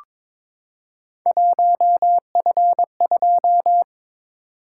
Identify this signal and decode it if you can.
Morse code, heard as a keyed tone, about 22 wpm, 700 Hz. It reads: 1F2
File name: practice / 22wpm / Three Letter-Number Combinations - 22wpm